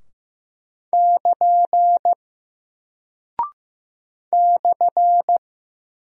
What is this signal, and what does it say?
Morse code, heard as a keyed tone, about 15 wpm, 700 Hz. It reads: <KN> /
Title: Single Character (Letter:Number:Punctuation:Prosign) Alt 15wpm